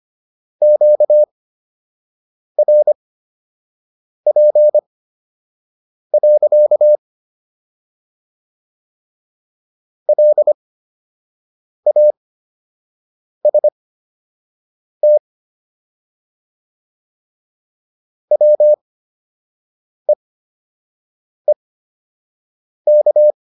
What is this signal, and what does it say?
Morse code, heard as a keyed tone, about 25 wpm, spaced as for 6 wpm, 600 Hz. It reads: QRP. LAST WEEK